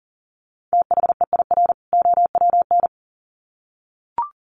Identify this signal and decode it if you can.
Morse code, heard as a keyed tone, about 40 wpm, 700 Hz. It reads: THEIR OWN
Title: Sets of 2 Words 40wpm